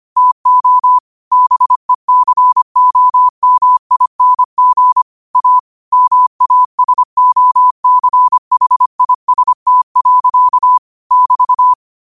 Morse code, heard as a keyed tone, about 25 wpm, 1000 Hz. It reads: TO BECOMING A MASOCHIST. <BT>
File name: qotdfiles / 25wpm